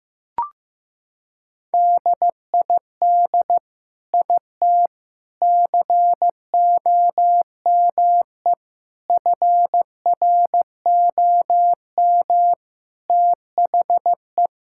Morse code, heard as a keyed tone, about 16 wpm, 700 Hz. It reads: DID IT COME FROM THE
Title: Sentences from Top 1000 Words - Review Part 5 - 15wpm